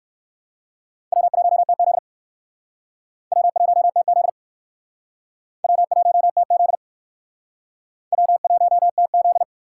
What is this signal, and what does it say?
Morse code, heard as a keyed tone, about 57 wpm, 700 Hz. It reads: W1TZ W1TZ W1TZ W1TZ